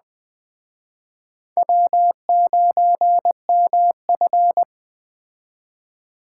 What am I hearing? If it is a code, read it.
Morse code, heard as a keyed tone, about 20 wpm, 700 Hz. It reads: W9MF